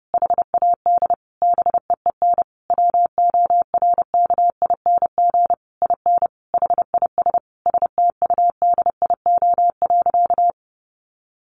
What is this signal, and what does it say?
Morse code, heard as a keyed tone, about 30 wpm, 700 Hz. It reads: HAD BEEN WORKING IN HIS STUDIO.
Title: Sentences from Top 1200 Words 30wpm